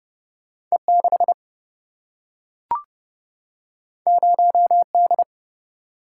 Morse code, heard as a keyed tone, about 30 wpm, 700 Hz. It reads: E6 0D
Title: Single-Letter Single-Number 30wpm